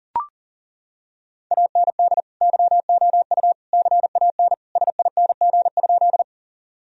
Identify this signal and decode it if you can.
Morse code, heard as a keyed tone, about 40 wpm, 700 Hz. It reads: AND YOU CAN SING?
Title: Sentences from Top 1000 Words - Review Part 2 - 40wpm